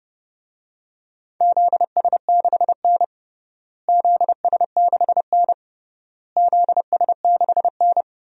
Morse code, heard as a keyed tone, about 30 wpm, 700 Hz. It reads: ZS6N ZS6N ZS6N
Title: DXCC-100 Call Signs - 4 Characters - Repeated 3x - 30wpm